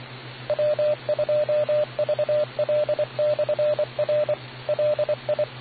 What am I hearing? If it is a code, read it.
Morse code, heard as a keyed tone, about 24 wpm, 600 Hz, with radio noise. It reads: W2VL/R LI